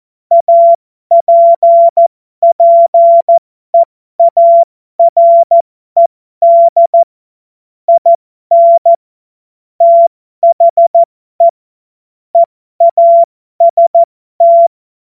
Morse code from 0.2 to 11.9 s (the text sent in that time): APPEARED IN THE